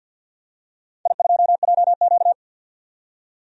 Morse code, heard as a keyed tone, about 45 wpm, 700 Hz. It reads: I2JQ
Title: DXCC-100 Call Signs - 4 Characters - 50wpm